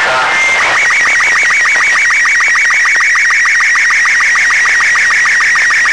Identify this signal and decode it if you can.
SSTV (slow-scan television), Robot 36, with speech.